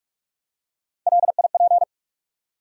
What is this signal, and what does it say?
Morse code, heard as a keyed tone, about 44 wpm, 700 Hz. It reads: LIP
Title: Three Letter Words - 45wpm